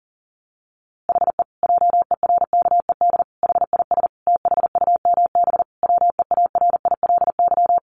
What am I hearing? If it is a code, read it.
Morse code, heard as a keyed tone, about 40 wpm, 700 Hz. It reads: HE JERKED HIS THUMB WEARILY
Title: A Sound Of Thunder 40wpm